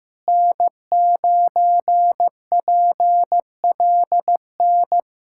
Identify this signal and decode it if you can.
Morse code, heard as a keyed tone, about 15 wpm, 700 Hz. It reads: N9PLN
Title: USA Callsign DE Callsign - 5 Characters - 15wpm